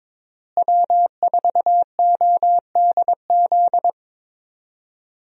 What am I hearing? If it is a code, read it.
Morse code, heard as a keyed tone, about 22 wpm, 700 Hz. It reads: W4ODZ
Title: USA Call Signs - 5 Characters - Repeated 3x - 22wpm